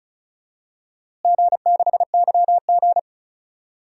Morse code, heard as a keyed tone, about 34 wpm, 700 Hz. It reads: G6YG
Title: DXCC-100 Call Signs - 4 Characters - Repeated 2x - 35wpm